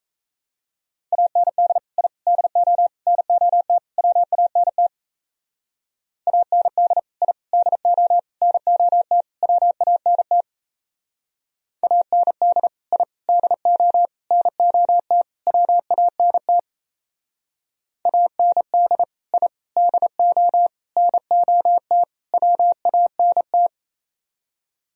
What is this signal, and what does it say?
Morse code, heard as a keyed tone, about 36 wpm, 700 Hz. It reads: AND I DO NOT WANT AND I DO NOT WANT AND I DO NOT WANT AND I DO NOT WANT